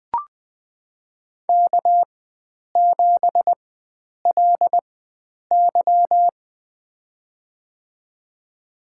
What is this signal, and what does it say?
Morse code, heard as a keyed tone, about 20 wpm, spaced as for 9 wpm, 700 Hz. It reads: K7LY
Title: USA Call Signs - 4 Characters - 4x Character Spacing - 20wpm